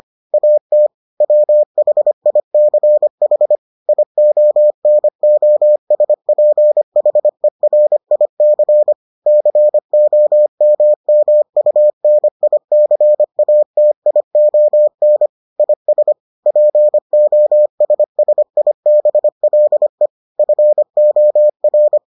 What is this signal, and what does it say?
Morse code, heard as a keyed tone, about 25 wpm, 600 Hz. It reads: AT WHICH IONOSPHERIC COMMUNICATION IS POSSIBLE FOR